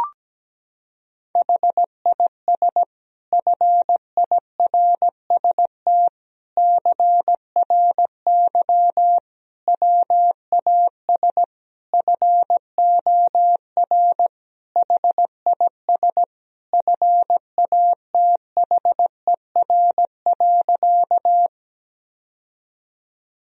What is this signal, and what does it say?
Morse code, heard as a keyed tone, about 17 wpm, 700 Hz. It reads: HIS FIRST CRY WAS FOR HIS FATHER.
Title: Sentences from Top 1000 Words - Review Part 2 - 17wpm